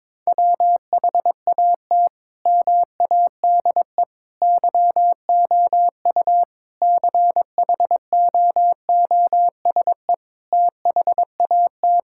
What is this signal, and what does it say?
Morse code, heard as a keyed tone, about 22 wpm, 700 Hz. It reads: WHAT MADE YOU CHOOSE THAT
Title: Sentences from Top 1000 Words - Review Part 5 - 22wpm